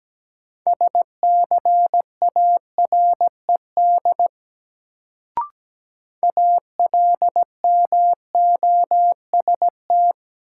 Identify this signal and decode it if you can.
Morse code, heard as a keyed tone, about 17 wpm, 700 Hz. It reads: SCARED ALMOST
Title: Sentences from Top 700 Words - New Word Drill - 17wpm